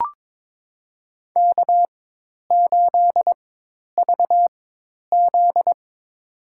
Morse code, heard as a keyed tone, about 22 wpm, spaced as for 10 wpm, 700 Hz. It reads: K8VZ
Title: USA Call Signs - 4x Character Spacing - 22wpm